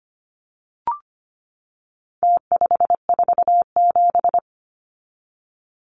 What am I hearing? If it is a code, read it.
Morse code, heard as a keyed tone, about 25 wpm, 700 Hz. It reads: T547
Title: Numbers - Contest Style - 4-Digit 25wpm